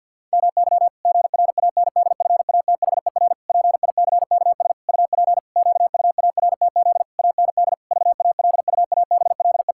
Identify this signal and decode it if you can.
Morse code, heard as a keyed tone, about 49 wpm, 700 Hz. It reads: MY GRANDFATHER PICKS UP QUARTZ AND VALUABLE
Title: Pangram Challenge v2 - 50wpm